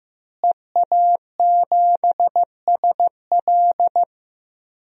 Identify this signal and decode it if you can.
Morse code, heard as a keyed tone, about 15 wpm, 700 Hz. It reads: EA7SL
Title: DXCC-100 Call Signs - 5 Characters - Repeated 3x - 15wpm